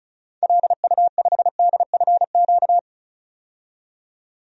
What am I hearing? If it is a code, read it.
Morse code, heard as a keyed tone, about 36 wpm, 700 Hz. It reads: LU5DFQ
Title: DXCC-100 Call Signs - 6 Characters - 35wpm